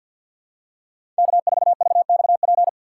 Morse code, heard as a keyed tone, about 52 wpm, 700 Hz. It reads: K4VXP
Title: USA Call Signs - 5 Characters - 8x Character Spacing - 50wpm